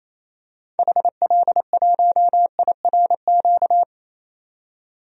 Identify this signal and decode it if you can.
Morse code, heard as a keyed tone, about 28 wpm, 700 Hz. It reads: HL1IRQ
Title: DXCC-100 Call Signs - 6 Characters - 28wpm